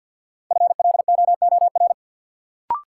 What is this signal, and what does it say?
Morse code, heard as a keyed tone, about 53 wpm, 700 Hz. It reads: FLOOR
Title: Sets of 3 Words - New Word Drill - 50wpm